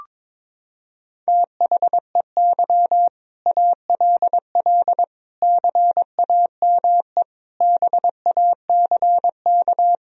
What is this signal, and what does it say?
Morse code, heard as a keyed tone, about 21 wpm, 700 Hz. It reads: THEY ALL CAME BACK
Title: Sentences from Top 1000 Words - Review Part 1 - 22wpm